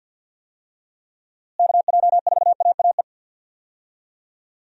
Morse code, heard as a keyed tone, about 46 wpm, 700 Hz. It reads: KJ4AAE
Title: USA Call Signs - 6 Characters - Repeated 2x - 50wpm